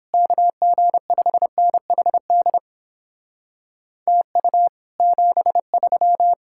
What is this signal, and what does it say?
Morse code, heard as a keyed tone, about 29 wpm, 700 Hz. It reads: KG5NHD TU 73